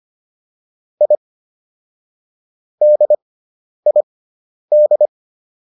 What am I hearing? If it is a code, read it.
Morse code, heard as a keyed tone, about 25 wpm, spaced as for 10 wpm, 600 Hz. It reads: I DID